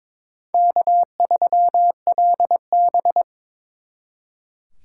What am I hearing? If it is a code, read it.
Morse code, heard as a keyed tone, about 26 wpm, 700 Hz. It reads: K3LB
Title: Speed Racing - USA 1x2 Call Signs 22wpm